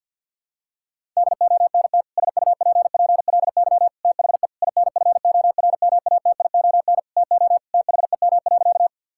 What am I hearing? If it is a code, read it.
Morse code, heard as a keyed tone, about 46 wpm, 700 Hz. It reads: DONT SUPPLY THE INFORMATION TO THEM.
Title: Sentences from Top 1000 Words - Review Part 5 - 50wpm